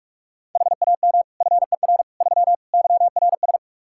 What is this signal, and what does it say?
Morse code, heard as a keyed tone, about 43 wpm, 700 Hz. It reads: HAM FER 3 YRS
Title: Practice QSO - Exchange 3 - More Info - 45wpm